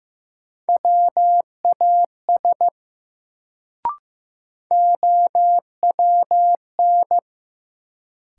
Morse code, heard as a keyed tone, about 15 wpm, 700 Hz. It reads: WAS OWN
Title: Sets of 2 Words - New Word Drill - 15wpm